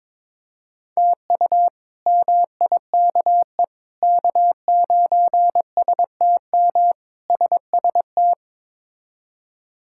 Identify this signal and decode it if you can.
Morse code, heard as a keyed tone, about 22 wpm, 700 Hz. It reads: TU MIKE K9STM SST
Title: K1USN SST QSO Practice - 22wpm